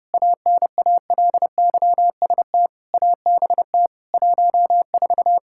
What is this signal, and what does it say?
Morse code, heard as a keyed tone, about 32 wpm, 700 Hz. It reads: ANALYST ABT 14
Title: Practice QSO - Exchange 3 - More Info - 30wpm